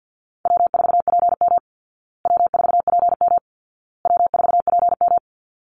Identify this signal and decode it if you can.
Morse code, heard as a keyed tone, about 50 wpm, 700 Hz. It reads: W4PM W4PM W4PM
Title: Easy Call Signs Repeated 3x - 50wpm